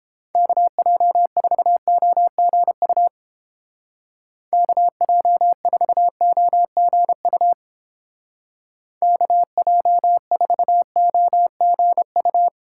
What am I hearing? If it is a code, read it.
Morse code, heard as a keyed tone, about 29 wpm, 700 Hz. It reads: KJ4OGU KJ4OGU KJ4OGU